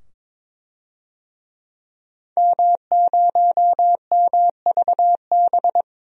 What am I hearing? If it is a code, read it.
Morse code, heard as a keyed tone, about 21 wpm, 700 Hz. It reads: M0MVB
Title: DXCC-100 Call Signs - 5 Characters - 22wpm